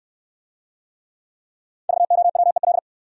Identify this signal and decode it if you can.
Morse code, heard as a keyed tone, about 64 wpm, 700 Hz. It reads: VOWEL